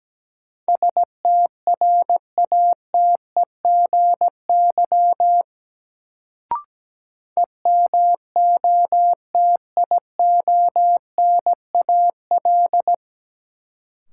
Morse code, heard as a keyed tone, about 17 wpm, 700 Hz. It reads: STRATEGY EMOTIONAL
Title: Sentences from Top 1000 Words - Review Part 5 - New Word Drill - 17wpm